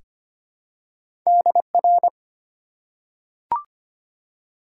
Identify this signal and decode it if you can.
Morse code, heard as a keyed tone, about 25 wpm, 700 Hz. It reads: DR
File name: Top 90 QSO Elements - 4x Character Spacing - 25wpm